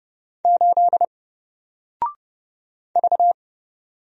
Morse code, heard as a keyed tone, about 31 wpm, 700 Hz. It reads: 8 V
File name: Single-Letter-Number-30wpm